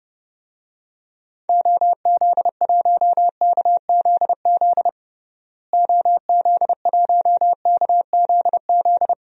Morse code, heard as a keyed tone, about 32 wpm, 700 Hz. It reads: OZ1KZZ OZ1KZZ
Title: DXCC-100 Call Signs - 6 Characters - Repeated 2x - 30wpm